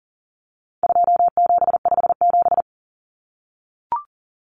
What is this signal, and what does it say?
Morse code, heard as a keyed tone, about 40 wpm, 700 Hz. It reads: 2757
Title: Common Cut Numbers - 4-Digit 40wpm